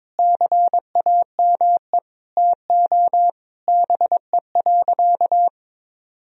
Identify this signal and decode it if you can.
Morse code, heard as a keyed tone, about 22 wpm, 700 Hz. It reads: CAME TO BE.
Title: Sentences from Top 1000 Words - Review Part 3 - 22wpm